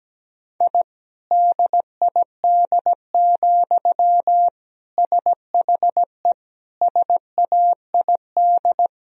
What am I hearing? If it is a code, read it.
Morse code, heard as a keyed tone, about 17 wpm, 700 Hz. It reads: I DID, SHE SAID
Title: Sentences from Top 1000 Words - Review Part 4 - 17wpm